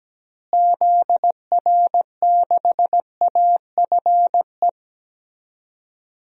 Morse code, heard as a keyed tone, about 17 wpm, 700 Hz. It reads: ZR6AFE
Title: DXCC-100 Call Signs - 6 Characters - Repeated 2x - 17wpm